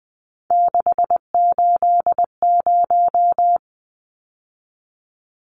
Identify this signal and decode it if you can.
Morse code, heard as a keyed tone, about 20 wpm, 700 Hz. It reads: 680
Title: Numbers - 3-Digit 20wpm